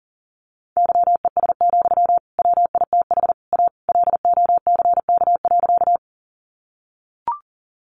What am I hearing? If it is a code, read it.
Morse code, heard as a keyed tone, about 40 wpm, 700 Hz. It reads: YES, WITH A LOCK.
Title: Sentences from Top 1300 Words 40wpm